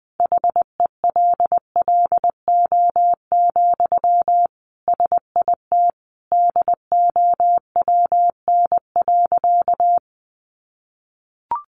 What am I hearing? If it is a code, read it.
Morse code, heard as a keyed tone, about 20 wpm, 700 Hz. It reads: HELLO, SIT DOWN.
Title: Sentences from Top 1100 Words 20wpm